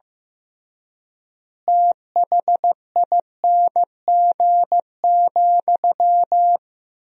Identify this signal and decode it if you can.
Morse code, heard as a keyed tone, about 15 wpm, 700 Hz. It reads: THING,
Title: 3x Word-Spacing - Sentences from Top 200 Words - 15wpm